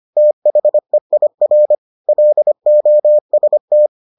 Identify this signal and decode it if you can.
Morse code, heard as a keyed tone, about 25 wpm, 600 Hz. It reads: THEIR LOST